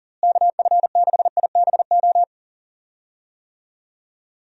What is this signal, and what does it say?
Morse code, heard as a keyed tone, about 37 wpm, 700 Hz. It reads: KF6IBO
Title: USA Call Signs - 6 Characters - Repeated 2x - 40wpm